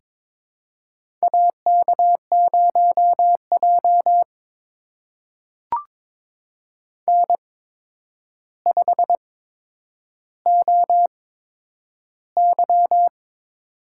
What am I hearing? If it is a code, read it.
Morse code, heard as a keyed tone, about 22 wpm, spaced as for 9 wpm, 700 Hz. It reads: AK0J N5OY
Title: USA Call Signs - 4 Characters - 8x Character Spacing - 22wpm